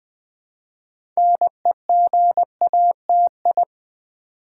Morse code, heard as a keyed tone, about 20 wpm, 700 Hz. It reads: NEGATI
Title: Build A Word - 3 Letters At A Time - 8-Letter Words - 20wpm